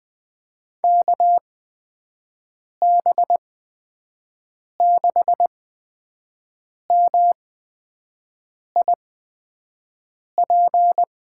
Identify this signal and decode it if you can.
Morse code, heard as a keyed tone, about 20 wpm, spaced as for 5 wpm, 700 Hz. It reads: KB6MIP
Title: USA Call Signs - 6 Characters - 8x Character Spacing - 20wpm